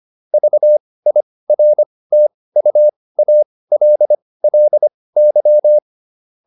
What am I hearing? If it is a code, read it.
Morse code, heard as a keyed tone, about 25 wpm, spaced as for 18 wpm, 600 Hz. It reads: VIRTUALLY